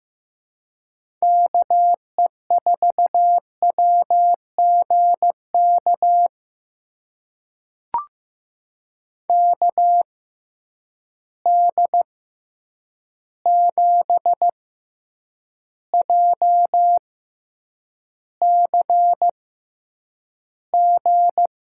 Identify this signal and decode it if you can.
Morse code, heard as a keyed tone, about 15 wpm, spaced as for 8 wpm, 700 Hz. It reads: KE4WGK KD7JCG
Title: USA Call Signs - 6 Characters - 6x Character Spacing - 15wpm